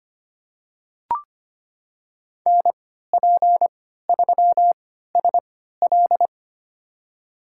Morse code, heard as a keyed tone, about 25 wpm, spaced as for 14 wpm, 700 Hz. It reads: NP3SL
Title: USA Call Signs - 5 Characters - 3x Character Spacing - 25wpm